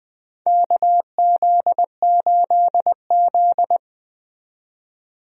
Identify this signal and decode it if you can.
Morse code, heard as a keyed tone, about 20 wpm, 700 Hz. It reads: KZ8Z
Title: USA Call Signs - 4 Characters - 20wpm